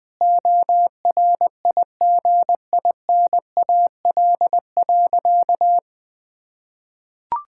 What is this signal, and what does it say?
Morse code, heard as a keyed tone, about 19 wpm, 700 Hz. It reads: ORIGINAL.
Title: Sentences from Top 1000 Words - Review Part 6 - 20wpm